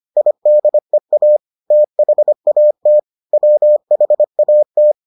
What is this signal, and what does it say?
Morse code, heard as a keyed tone, about 25 wpm, 600 Hz. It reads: IDEA THAT WHAT